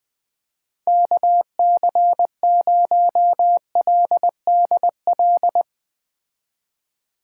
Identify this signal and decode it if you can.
Morse code, heard as a keyed tone, about 21 wpm, 700 Hz. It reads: KC0LDL
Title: Speed Racing - USA Call Signs 17wpm